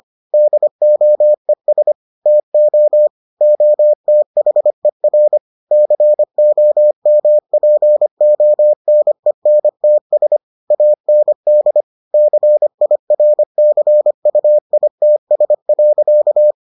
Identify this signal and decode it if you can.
Morse code, heard as a keyed tone, about 25 wpm, 600 Hz. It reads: DOES TO OTHER COMPONENTS AND CIRCUITS.